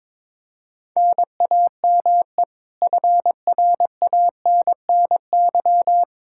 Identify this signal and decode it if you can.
Morse code, heard as a keyed tone, about 21 wpm, 700 Hz. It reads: NAME FRANNY
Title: Practice QSO - Exchange 1 - Beginning - 22wpm